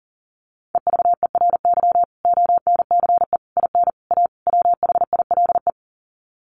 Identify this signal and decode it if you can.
Morse code, heard as a keyed tone, about 40 wpm, 700 Hz. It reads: EVERY ONCE IN A WHILE
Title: Sets of 5 Words 40wpm